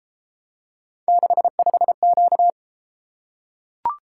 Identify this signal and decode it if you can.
Morse code, heard as a keyed tone, about 33 wpm, 700 Hz. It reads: 65Q